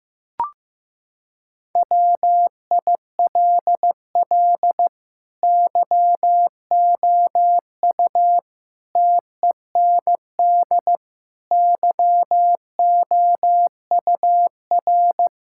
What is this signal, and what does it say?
Morse code, heard as a keyed tone, about 15 wpm, 700 Hz. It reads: WILL YOU TEND YOUR
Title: Sentences from Top 1000 Words - Review Part 5 - 15wpm